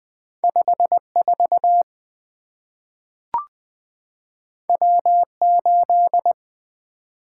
Morse code, heard as a keyed tone, about 20 wpm, 700 Hz. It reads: 54 W8
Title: Two Letter-Number Combinations - 20wpm